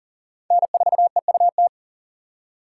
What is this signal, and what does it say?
Morse code, heard as a keyed tone, about 40 wpm, 700 Hz. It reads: N4EUT
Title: USA Call Signs - 4x Character Spacing - 40wpm